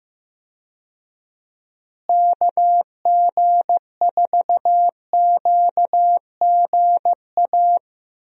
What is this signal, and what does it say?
Morse code, heard as a keyed tone, about 15 wpm, 700 Hz. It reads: KG4QGA
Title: USA Call Signs - 6 Characters - Repeated 3x - 15wpm